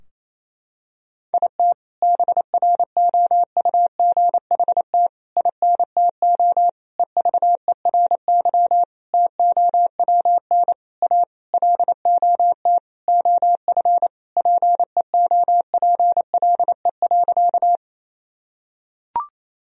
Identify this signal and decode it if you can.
Morse code, heard as a keyed tone, about 28 wpm, 700 Hz. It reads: IT BROUGHT INTO EVERY TOWN A LOT OF PEOPLE.
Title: Sentences from Top 1000 Words - Review Part 3 - 28wpm